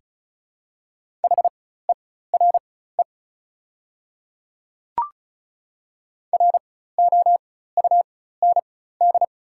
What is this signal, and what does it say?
Morse code, heard as a keyed tone, about 34 wpm, spaced as for 16 wpm, 700 Hz. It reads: HERE ROUND
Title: 4x Character Spacing - Top 300 Words - 35wpm